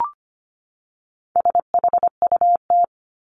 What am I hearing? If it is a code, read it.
Morse code, heard as a keyed tone, about 25 wpm, 700 Hz. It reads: SHUT
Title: Top 1751-2000 Words 25wpm